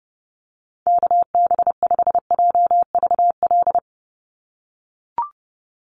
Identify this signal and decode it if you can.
Morse code, heard as a keyed tone, about 30 wpm, 700 Hz. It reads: KB5JVL
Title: USA Call Signs 30wpm